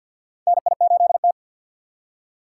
Morse code, heard as a keyed tone, about 52 wpm, 700 Hz. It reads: NI8T
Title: USA Call Signs - 6x Character Spacing - 50wpm